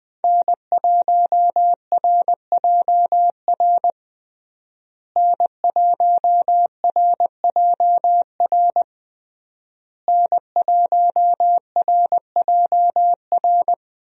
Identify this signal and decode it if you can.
Morse code, heard as a keyed tone, about 21 wpm, 700 Hz. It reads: N1RJR N1RJR N1RJR
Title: USA Call Signs - 5 Characters - Repeated 3x - 20wpm